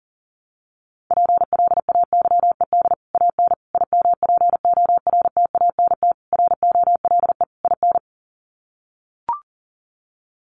Morse code, heard as a keyed tone, about 40 wpm, 700 Hz. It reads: PLAYED AN IMPORTANT ROLE IN
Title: Sets of 5 Words Encore2 40wpm